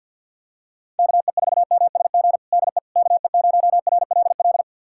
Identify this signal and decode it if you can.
Morse code, heard as a keyed tone, about 49 wpm, 700 Hz. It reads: KE4MSG DE KE0RLL
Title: USA Callsign DE Callsign - 6 Characters - 50wpm